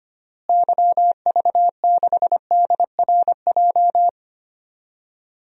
Morse code, heard as a keyed tone, about 25 wpm, 700 Hz. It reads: YV6DRJ
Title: DXCC-100 Call Signs - 6 Characters - Repeated 2x - 25wpm